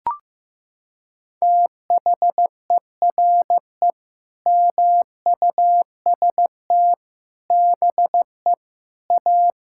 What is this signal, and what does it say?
Morse code, heard as a keyed tone, about 16 wpm, 700 Hz. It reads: THERE MUST BE A
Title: Sentences from Top 1000 Words - Review Part 6 - 15wpm